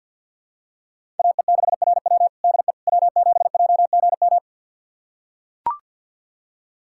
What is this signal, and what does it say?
Morse code, heard as a keyed tone, about 45 wpm, 700 Hz. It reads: AE6RW DE W7JGM
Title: USA Callsign DE Callsign - 5 Characters - 50wpm